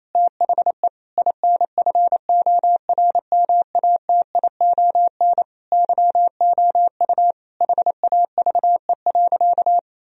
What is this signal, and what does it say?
Morse code, heard as a keyed tone, about 28 wpm, 700 Hz. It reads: THE INFORMATION YOU HAVE.
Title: Sentences from Top 1000 Words - Review Part 3 - 28wpm